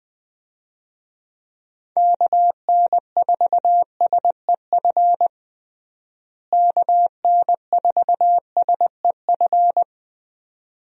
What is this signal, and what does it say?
Morse code, heard as a keyed tone, about 21 wpm, 700 Hz. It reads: KN4SEF KN4SEF
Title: USA Call Signs - 6 Characters - Repeated 2x - 20wpm